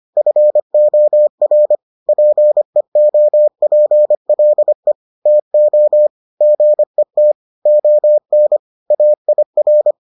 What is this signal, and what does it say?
Morse code, heard as a keyed tone, about 25 wpm, 600 Hz. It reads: FOR PEOPLE TO GET ON AIR